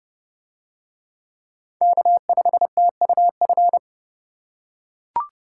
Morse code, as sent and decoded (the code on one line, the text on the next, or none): -.- ..... - ..- ..-.
K5TUF